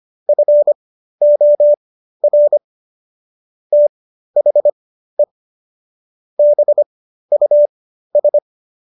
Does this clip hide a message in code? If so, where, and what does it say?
Yes: Morse code from 0.3 to 8.4 s, heard as a keyed tone, about 25 wpm, spaced as for 13 wpm, 590 Hz. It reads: FOR THE BUS